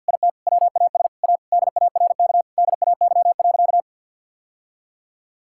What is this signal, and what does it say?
Morse code, heard as a keyed tone, about 50 wpm, 700 Hz. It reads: IT WAS A DARK DAY.
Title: Sentences from Top 1000 Words - Review Part 5 - 50wpm